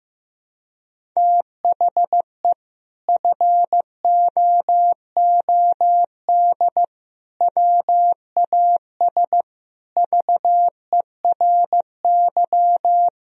Morse code, heard as a keyed tone, about 15 wpm, 700 Hz. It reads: THE FOOD WAS VERY
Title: Sentences from Top 1000 Words - Review Part 1 - 15wpm